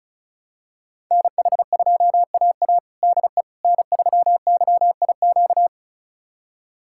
Morse code, heard as a keyed tone, about 36 wpm, 700 Hz. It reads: NH2AA DE N3YIQ
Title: USA Callsign DE Callsign - 5 Characters - 35wpm